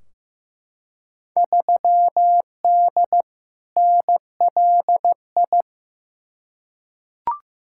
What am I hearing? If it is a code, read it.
Morse code, heard as a keyed tone, about 16 wpm, 700 Hz. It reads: 3D NLI
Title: Field Day Exchanges Encore - 15wpm